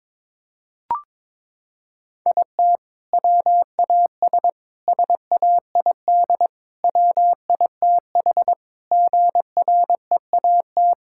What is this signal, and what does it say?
Morse code, heard as a keyed tone, about 22 wpm, 700 Hz. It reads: IT WAS SAID WITH GREAT
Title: Sentences from Top 1600 Words 22wpm